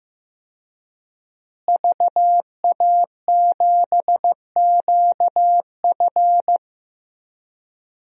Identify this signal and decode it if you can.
Morse code, heard as a keyed tone, about 15 wpm, 700 Hz. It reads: VA7QF
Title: DXCC-100 Call Signs - 5 Characters - Repeated 3x - 15wpm